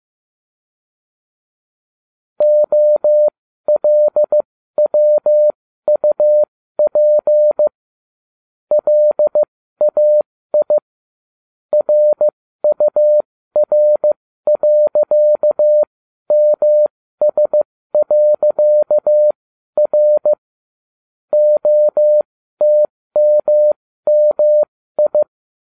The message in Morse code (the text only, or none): OLWUP LAI RUR.MS.R OTMMI